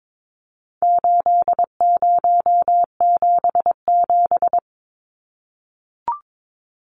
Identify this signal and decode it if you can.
Morse code, heard as a keyed tone, about 22 wpm, 700 Hz. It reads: 8077
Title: Numbers - 4-Digit 22wpm